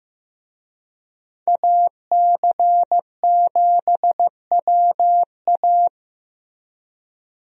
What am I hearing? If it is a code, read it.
Morse code, heard as a keyed tone, about 15 wpm, 700 Hz. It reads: AC7WA
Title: USA Call Signs - 5 Characters - Repeated 2x - 15wpm